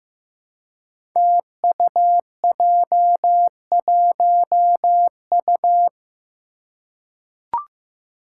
Morse code, heard as a keyed tone, about 15 wpm, 700 Hz. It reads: TUJ1U
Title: Five Challenging Letter-Number Combinations 15wpm